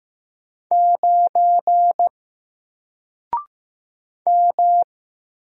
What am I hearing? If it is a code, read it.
Morse code, heard as a keyed tone, about 15 wpm, 700 Hz. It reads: 9 M
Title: Single Character (Letter:Number:Punctuation:Prosign) 15wpm